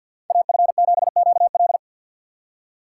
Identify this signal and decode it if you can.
Morse code, heard as a keyed tone, about 46 wpm, 700 Hz. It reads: AF7QL
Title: USA Call Signs - 5 Characters - 6x Character Spacing - 50wpm